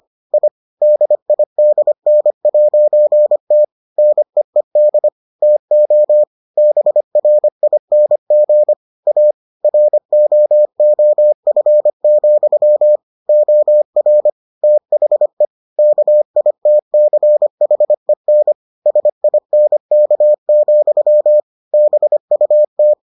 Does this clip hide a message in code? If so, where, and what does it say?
Yes: Morse code from 0.3 to 22.9 s, heard as a keyed tone, about 25 wpm, 600 Hz. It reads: I DIDN'T NEED TO BRING A ROOF, OR THE KITCHEN SINK, BUT